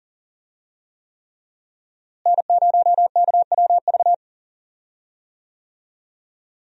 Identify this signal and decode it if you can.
Morse code, heard as a keyed tone, about 40 wpm, 700 Hz. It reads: N0KWV
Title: USA Call Signs - 5 Characters - Repeated 3x - 40wpm